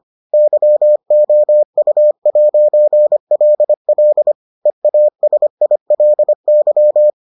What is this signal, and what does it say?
Morse code, heard as a keyed tone, about 25 wpm, 600 Hz. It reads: YOU'LL EASILY